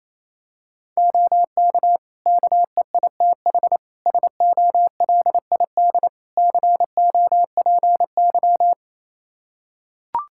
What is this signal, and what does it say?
Morse code, heard as a keyed tone, about 28 wpm, 700 Hz. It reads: OK KEITH SOLID COPY
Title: Practice QSO - Exchange 3 - More Info - 28wpm